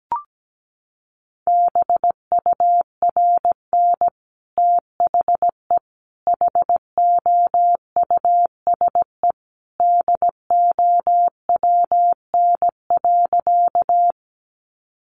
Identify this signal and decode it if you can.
Morse code, heard as a keyed tone, about 17 wpm, 700 Hz. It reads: BURN THE HOUSE DOWN.
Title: Sentences from Top 1300 Words 17wpm